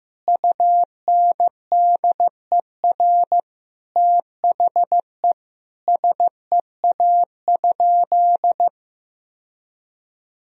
Morse code, heard as a keyed tone, about 15 wpm, 700 Hz. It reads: UNDER THE SEA?
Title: Sentences from Top 1000 Words - Review Part 1 - 15wpm